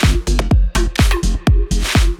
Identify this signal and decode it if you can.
There is no code.